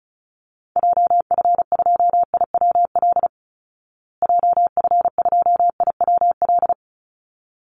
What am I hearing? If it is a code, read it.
Morse code, heard as a keyed tone, about 34 wpm, 700 Hz. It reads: JF2IWL JF2IWL
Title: Call Signs Repeated 3x - 35wpm